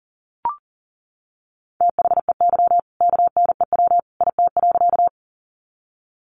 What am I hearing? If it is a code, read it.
Morse code, heard as a keyed tone, about 40 wpm, 700 Hz. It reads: THEY KNEW IT.
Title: Sentences from Top 400 Words 40wpm